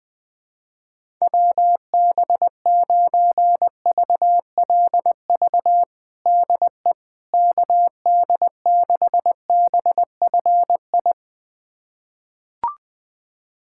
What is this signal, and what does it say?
Morse code, heard as a keyed tone, about 20 wpm, 700 Hz. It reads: WB9VLV DE KD6BFI
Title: USA Callsign DE Callsign 20wpm